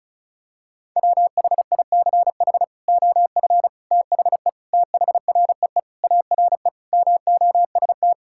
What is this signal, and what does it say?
Morse code, heard as a keyed tone, about 34 wpm, 700 Hz. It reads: WHICH OF THE THREE ARE MOST
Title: Sentences from Top 1000 Words - Review Part 5 - 35wpm